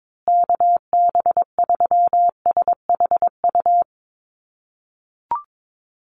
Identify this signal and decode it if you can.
Morse code, heard as a keyed tone, about 22 wpm, 700 Hz. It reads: KB3SHU
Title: USA Call Signs Repeated 3x 22wpm